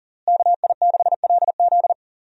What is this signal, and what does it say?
Morse code, heard as a keyed tone, about 40 wpm, 700 Hz. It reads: KI6LZ
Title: USA Call Signs - 5 Characters - Repeated 3x - 40wpm